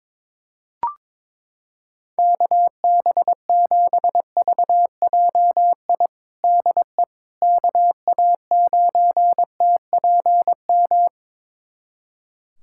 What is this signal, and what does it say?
Morse code, heard as a keyed tone, about 21 wpm, 700 Hz. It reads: KB7VJI DE KA9TPM
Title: USA Callsign DE Callsign - 6 Characters - 22wpm